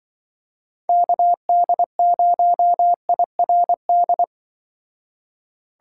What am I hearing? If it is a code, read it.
Morse code, heard as a keyed tone, about 24 wpm, 700 Hz. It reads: KD0IRD
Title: Speed Racing - USA 2x3 Call Signs 20wpm